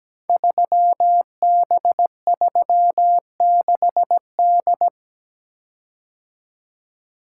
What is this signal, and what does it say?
Morse code, heard as a keyed tone, about 17 wpm, 700 Hz. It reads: 3B36D
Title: Code Groups - 5 Letter:Numbers - 17wpm